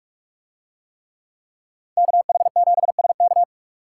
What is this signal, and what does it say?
Morse code, heard as a keyed tone, about 44 wpm, 700 Hz. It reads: KH7SK